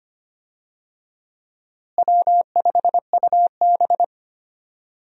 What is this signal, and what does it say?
Morse code, heard as a keyed tone, about 25 wpm, 700 Hz. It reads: W5UB
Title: USA Call Signs - 4 Characters - Repeated 2x - 25wpm